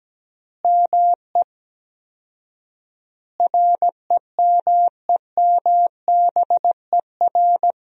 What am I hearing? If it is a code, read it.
Morse code, heard as a keyed tone, about 17 wpm, 700 Hz. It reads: ME REMEMBER
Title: 4x Word-Spacing - Sentences from Top 400 Words - 17wpm